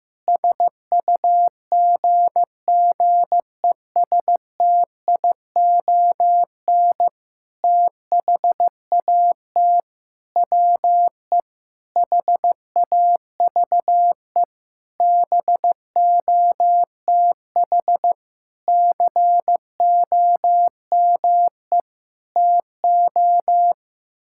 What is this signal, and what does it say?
Morse code, heard as a keyed tone, about 15 wpm, 700 Hz. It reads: SUGGESTION THAT WE HAVE BOTH COME TO
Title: Sentences from Top 1600 Words 15wpm